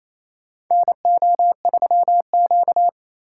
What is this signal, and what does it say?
Morse code, heard as a keyed tone, about 27 wpm, 700 Hz. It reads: NO3Q
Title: USA Call Signs - 4 Characters - 3x Character Spacing - 28wpm